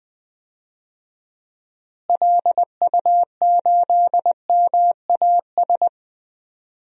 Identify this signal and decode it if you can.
Morse code, heard as a keyed tone, about 20 wpm, 700 Hz. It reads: LU8MAS
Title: DXCC-100 Call Signs - 6 Characters - Repeated 2x - 20wpm